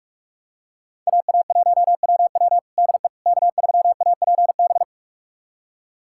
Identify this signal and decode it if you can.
Morse code, heard as a keyed tone, about 47 wpm, 700 Hz. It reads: AA1WW DE K3APB
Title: USA Callsign DE Callsign - 5 Characters - 45wpm